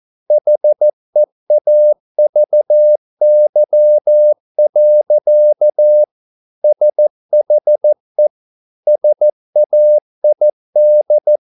Morse code, heard as a keyed tone, about 14 wpm, 600 Hz. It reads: HEAVY. SHE SAID